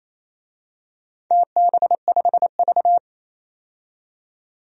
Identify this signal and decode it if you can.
Morse code, heard as a keyed tone, about 28 wpm, 700 Hz. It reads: TB5V
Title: DXCC-100 Call Signs - 4 Characters - 28wpm